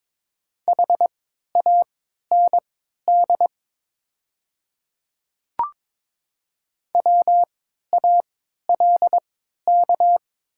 Morse code, heard as a keyed tone, about 22 wpm, spaced as for 13 wpm, 700 Hz. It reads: HAND WALK
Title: 3x Character Spacing - Top 300 Words - 22wpm